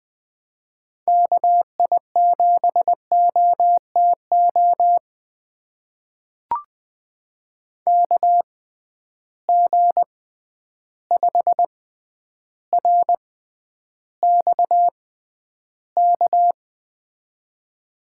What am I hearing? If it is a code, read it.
Morse code, heard as a keyed tone, about 20 wpm, spaced as for 10 wpm, 700 Hz. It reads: KI7OTO KG5RXK
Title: USA Call Signs - 6 Characters - 6x Character Spacing - 20wpm